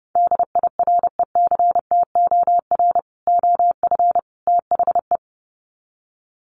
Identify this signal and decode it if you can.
Morse code, heard as a keyed tone, about 30 wpm, 700 Hz. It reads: DIRECTOR OF THE
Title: Sets of 3 Words 30wpm